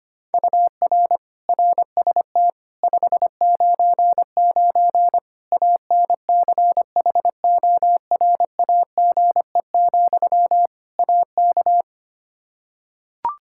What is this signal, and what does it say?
Morse code, heard as a keyed tone, about 25 wpm, 700 Hz. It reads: UR RST 599 ANCHORAGE, AK
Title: K1USN SST QSO Practice 25wpm